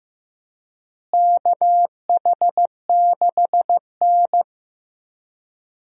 Morse code, heard as a keyed tone, about 15 wpm, 700 Hz. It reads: KH6N
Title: DXCC-100 Call Signs - 4 Characters - Repeated 3x - 15wpm